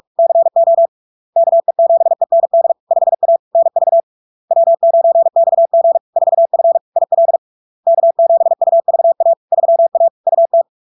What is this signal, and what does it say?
Morse code, heard as a keyed tone, about 46 wpm, 670 Hz. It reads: CO KE7END 5A NV W9XG 4F IL K7UVA 3A UT
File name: LARK-FD-2022-Log-in-CW-45wpm-670-Hz